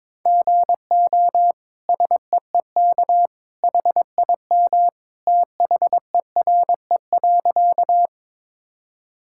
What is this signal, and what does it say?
Morse code, heard as a keyed tone, about 22 wpm, 700 Hz. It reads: GO SEEK HIM THERE.
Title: Sentences from Top 1600 Words 22wpm